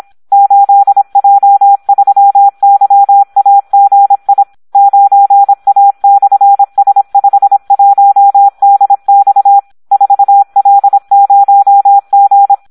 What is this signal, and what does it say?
Morse code, heard as a keyed tone, about 26 wpm, 800 Hz, with radio noise. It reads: 8J3YAGI 9A/S51DX 4L0G
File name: Audio practica CW / SD 1 26